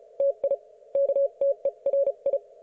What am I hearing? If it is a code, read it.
Morse code, heard as a keyed tone, about 34 wpm, spaced as for 30 wpm, 560 Hz. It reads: TI KTERI